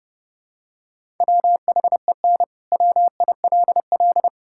Morse code, heard as a keyed tone, about 32 wpm, 700 Hz. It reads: WHEN WILL
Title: Sentences from Top 1000 Words - Review Part 5 - 30wpm